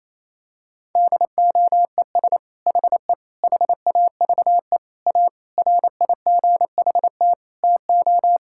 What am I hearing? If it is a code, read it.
Morse code, heard as a keyed tone, about 28 wpm, 700 Hz. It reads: DOES HE HAVE A RIGHT TO
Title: Sentences from Top 1000 Words - Review Part 5 - 28wpm